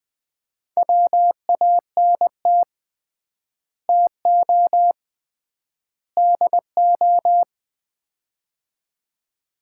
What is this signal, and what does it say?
Morse code, heard as a keyed tone, about 21 wpm, 700 Hz. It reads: WANT TO DO
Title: 3x Word-Spacing - Sets of 3 Words - 20wpm